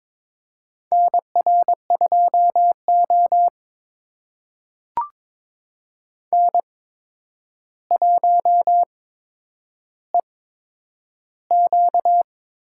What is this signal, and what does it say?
Morse code, heard as a keyed tone, about 22 wpm, spaced as for 9 wpm, 700 Hz. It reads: NR2O N1EQ
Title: USA Call Signs - 4 Characters - 8x Character Spacing - 22wpm